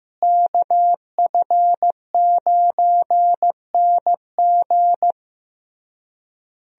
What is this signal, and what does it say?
Morse code, heard as a keyed tone, about 15 wpm, 700 Hz. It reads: KF9NG
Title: USA Call Signs - 5 Characters - Repeated 3x - 15wpm